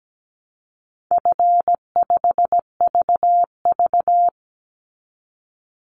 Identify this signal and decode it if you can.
Morse code, heard as a keyed tone, about 17 wpm, 700 Hz. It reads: F5VV
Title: Call Signs - 17wpm